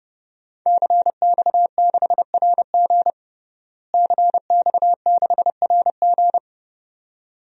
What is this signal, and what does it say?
Morse code, heard as a keyed tone, about 30 wpm, 700 Hz. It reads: CX6RG CX6RG
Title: DXCC-100 Call Signs - 5 Characters - Repeated 2x - 30wpm